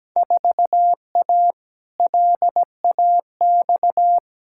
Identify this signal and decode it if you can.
Morse code, heard as a keyed tone, about 17 wpm, 700 Hz. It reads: 4A LAX
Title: Field Day Exchanges - 17wpm